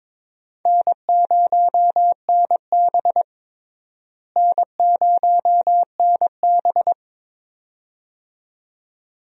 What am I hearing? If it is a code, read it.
Morse code, heard as a keyed tone, about 22 wpm, 700 Hz. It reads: N0NB N0NB
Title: USA Call Signs - 4 Characters - Repeated 2x - 22wpm